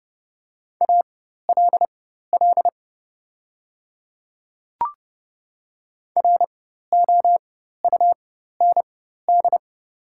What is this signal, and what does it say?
Morse code, heard as a keyed tone, about 30 wpm, spaced as for 14 wpm, 700 Hz. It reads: ALL ROUND
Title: 4x Character Spacing - Top 200 Words - 30wpm